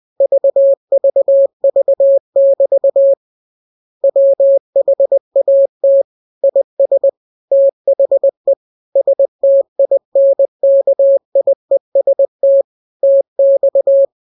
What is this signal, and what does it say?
Morse code, heard as a keyed tone, about 20 wpm, 550 Hz. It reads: VVV<BT> WHAT IS THE STINKIEST TX
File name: jokes / joke 27 20 0000